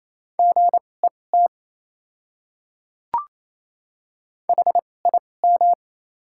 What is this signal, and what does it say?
Morse code, heard as a keyed tone, about 28 wpm, spaced as for 20 wpm, 690 Hz. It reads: GET HIM
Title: 2x Character Spacing - Top 200 Words - 28wpm